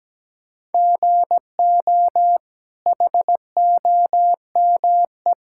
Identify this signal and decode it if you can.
Morse code, heard as a keyed tone, about 17 wpm, 700 Hz. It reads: GO HOME
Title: Sentences from Top 1600 Words 17wpm